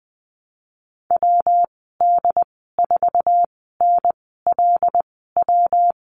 Morse code, heard as a keyed tone, about 20 wpm, spaced as for 15 wpm, 700 Hz. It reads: WD4NLW
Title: 173 Lesson 28g - Number 9 - Call Signs 20wpm